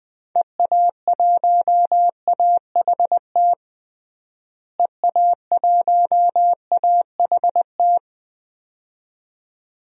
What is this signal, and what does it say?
Morse code, heard as a keyed tone, about 20 wpm, 700 Hz. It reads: EA1AHT EA1AHT
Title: DXCC-100 Call Signs - 6 Characters - Repeated 2x - 20wpm